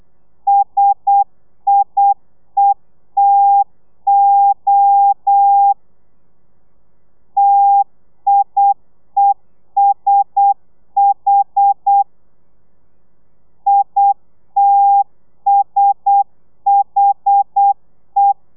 Morse code, heard as a keyed tone, about 8 wpm, 800 Hz. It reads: SIETO TIESH ITSHE
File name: telegrafia / i2tfj / Lez1 35